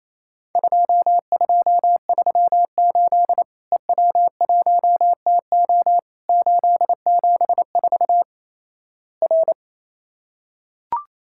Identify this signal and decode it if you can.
Morse code, heard as a keyed tone, about 28 wpm, 700 Hz. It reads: 2238 EW1TO 874 R
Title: Worked All Europe DX Contest - QTC Exchange - 28wpm